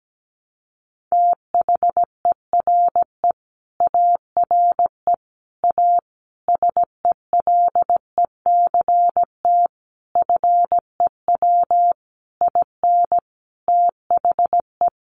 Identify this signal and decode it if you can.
Morse code, heard as a keyed tone, about 17 wpm, 700 Hz. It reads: THERE ARE A SELECT FEW IN THE
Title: Sentences from Top 1300 Words 17wpm